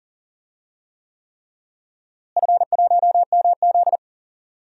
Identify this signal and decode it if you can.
Morse code, heard as a keyed tone, about 40 wpm, 700 Hz. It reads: F1MZ
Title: DXCC-100 Call Signs - 4 Characters - Repeated 3x - 40wpm